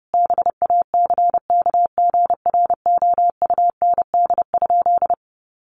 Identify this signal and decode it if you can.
Morse code, heard as a keyed tone, about 30 wpm, 700 Hz. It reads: BACKGROUND?
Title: Sentences from Top 1300 Words 30wpm